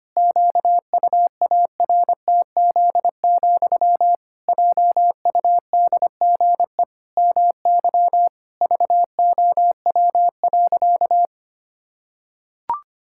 Morse code, heard as a keyed tone, about 25 wpm, 700 Hz. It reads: QUARTZ, JUDGE MY VOW.
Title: Pangram Challenge v2 - 25wpm